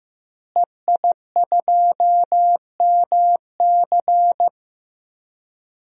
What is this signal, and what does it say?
Morse code, heard as a keyed tone, about 15 wpm, 700 Hz. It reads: EI2MC
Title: DXCC-100 Call Signs - 5 Characters - Repeated 2x - 15wpm